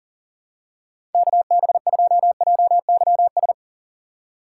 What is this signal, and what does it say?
Morse code, heard as a keyed tone, about 40 wpm, 700 Hz. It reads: KB2JYS
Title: USA Call Signs - 4x Character Spacing - 40wpm